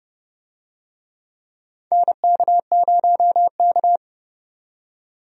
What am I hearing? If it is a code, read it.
Morse code, heard as a keyed tone, about 30 wpm, 700 Hz. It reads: NK0K
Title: USA Call Signs - 4 Characters - 30wpm